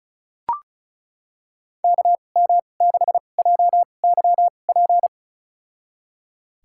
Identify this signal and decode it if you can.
Morse code, heard as a keyed tone, about 34 wpm, spaced as for 25 wpm, 700 Hz. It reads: KM6JYP
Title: USA Call Signs - 6 Characters - 2x Character Spacing - 35wpm